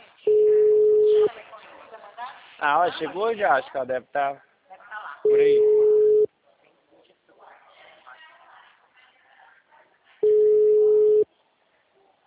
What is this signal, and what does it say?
Ringback tone (Europe, 425 Hz), with speech.